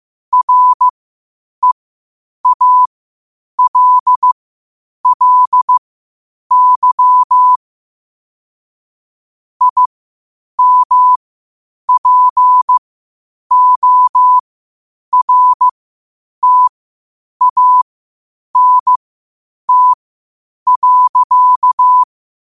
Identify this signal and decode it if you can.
Morse code, heard as a keyed tone, about 15 wpm, spaced as for 8 wpm, 1000 Hz. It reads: REALLY IMPORTANT.
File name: qotdfiles / 8wpm